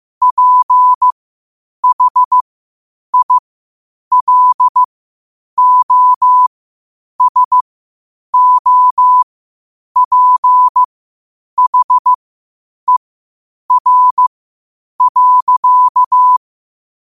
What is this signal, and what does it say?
Morse code, heard as a keyed tone, about 15 wpm, spaced as for 8 wpm, 1000 Hz. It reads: PHILOSOPHER.